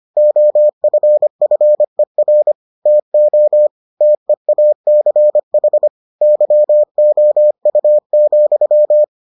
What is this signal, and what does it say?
Morse code, heard as a keyed tone, about 25 wpm, 600 Hz. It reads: OFFER TO TEACH YOU,